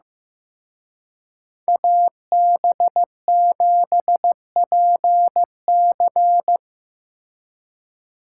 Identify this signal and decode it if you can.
Morse code, heard as a keyed tone, about 18 wpm, 700 Hz. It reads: AB7PC
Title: Speed Racing - USA 2x2 Call Signs 15wpm